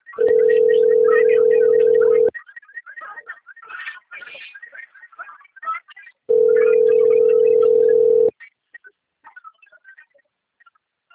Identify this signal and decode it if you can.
Ringback tone (North America), 440 + 480 Hz.
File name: rg-1600-unknown- - - .38288